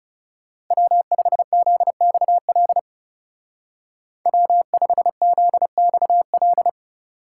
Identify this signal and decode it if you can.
Morse code, heard as a keyed tone, about 37 wpm, 700 Hz. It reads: W5ZXL W5ZXL
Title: Speed Racing - USA 1x3 Call Signs 30wpm